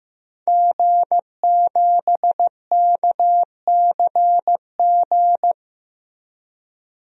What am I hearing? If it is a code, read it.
Morse code, heard as a keyed tone, about 15 wpm, 700 Hz. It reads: G7KCG
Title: DXCC-100 Call Signs - 5 Characters - Repeated 3x - 15wpm